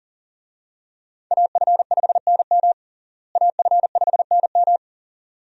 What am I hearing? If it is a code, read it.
Morse code, heard as a keyed tone, about 40 wpm, 700 Hz. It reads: AF5NM AF5NM
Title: USA Call Signs - 5 Characters - Repeated 2x - 40wpm